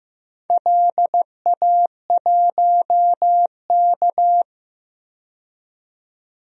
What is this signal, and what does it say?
Morse code, heard as a keyed tone, about 15 wpm, 700 Hz. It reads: LA1K
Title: DXCC-100 Call Signs - 4 Characters - 15wpm